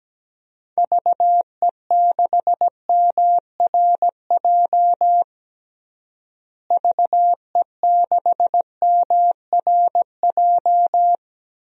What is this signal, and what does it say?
Morse code, heard as a keyed tone, about 17 wpm, 700 Hz. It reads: VE6MRJ VE6MRJ
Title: DXCC-100 Call Signs - 6 Characters - Repeated 2x - 17wpm